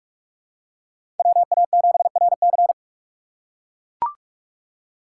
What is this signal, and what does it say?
Morse code, heard as a keyed tone, about 42 wpm, 700 Hz. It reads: WA7RC
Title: USA Call Signs - 2x Character Spacing - 45wpm